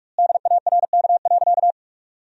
Morse code, heard as a keyed tone, about 46 wpm, 700 Hz. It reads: DARK.
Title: 2x Word-Spacing - Sentences from Top 500 Words - 45wpm